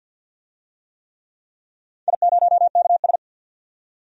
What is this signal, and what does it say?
Morse code, heard as a keyed tone, about 51 wpm, 700 Hz. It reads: I0KS
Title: DXCC-100 Call Signs - 4 Characters - 50wpm